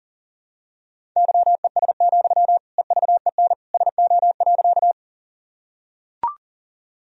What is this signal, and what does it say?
Morse code, heard as a keyed tone, about 40 wpm, 700 Hz. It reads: YES, EVEN SO.
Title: Sentences from Top 1000 Words - Review Part 1 - 40wpm